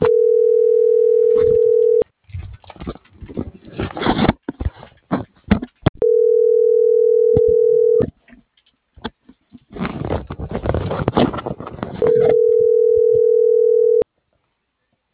Ringback tone (North America), 440 + 480 Hz.